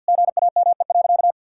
Morse code, heard as a keyed tone, about 52 wpm, 700 Hz. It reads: GAME.
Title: Sentences from Top 1000 Words - Review Part 6 - 50wpm